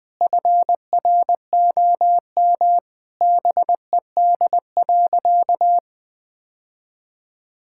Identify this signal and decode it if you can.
Morse code, heard as a keyed tone, about 20 wpm, 700 Hz. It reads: FROM BED.
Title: Sentences from Top 1000 Words - Review Part 6 - 20wpm